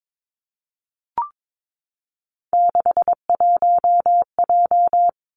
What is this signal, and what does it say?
Morse code, heard as a keyed tone, about 22 wpm, 700 Hz. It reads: 61J
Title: Three Challenging Letter-Number Combinations 22wpm